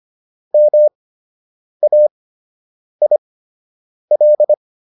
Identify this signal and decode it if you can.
Morse code, heard as a keyed tone, about 25 wpm, spaced as for 8 wpm, 600 Hz. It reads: MAIL